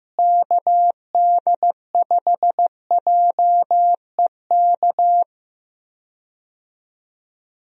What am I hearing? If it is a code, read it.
Morse code, heard as a keyed tone, about 16 wpm, 700 Hz. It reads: KD5JEK
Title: USA Call Signs - 6 Characters - Repeated 3x - 15wpm